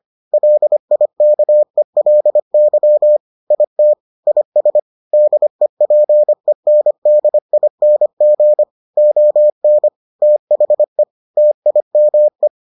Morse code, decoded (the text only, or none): LIKELY IT IS DEPENDING ON THE TIME